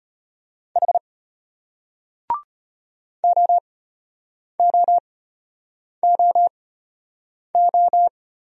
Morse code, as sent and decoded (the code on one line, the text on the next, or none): .... / --- --- --- ---
H OOOO